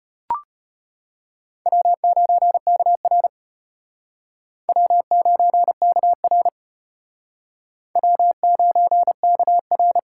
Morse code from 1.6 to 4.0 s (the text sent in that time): W9KR